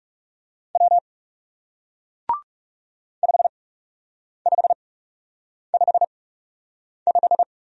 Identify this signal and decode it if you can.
Morse code, heard as a keyed tone, about 38 wpm, spaced as for 8 wpm, 700 Hz. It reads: W 5555